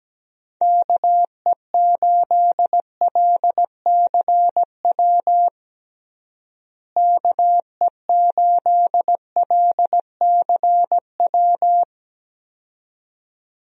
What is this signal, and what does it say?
Morse code, heard as a keyed tone, about 17 wpm, 700 Hz. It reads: KE8LCW KE8LCW
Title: USA Call Signs - 6 Characters - Repeated 2x - 17wpm